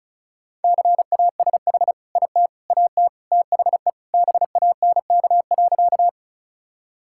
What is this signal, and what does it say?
Morse code, heard as a keyed tone, about 34 wpm, 700 Hz. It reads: CASH IT AT THE BANK.
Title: Sentences from Top 1000 Words - Review Part 5 - 35wpm